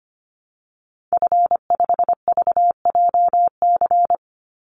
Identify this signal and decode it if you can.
Morse code, heard as a keyed tone, about 25 wpm, 700 Hz. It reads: F5VJC
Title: Call Signs Repeated 2x - 25wpm